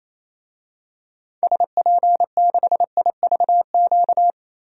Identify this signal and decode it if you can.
Morse code, heard as a keyed tone, about 28 wpm, 700 Hz. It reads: SP6IVQ
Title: DXCC-100 Call Signs - 6 Characters - Repeated 2x - 28wpm